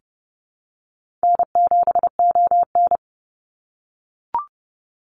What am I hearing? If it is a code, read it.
Morse code, heard as a keyed tone, about 30 wpm, 700 Hz. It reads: N7ON
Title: Easy Call Signs - 30wpm